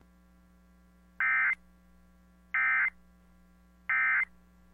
Emergency Alert System (SAME) tones, heard as harsh buzzing bursts. It is heard from 1.2 to 4.2 s.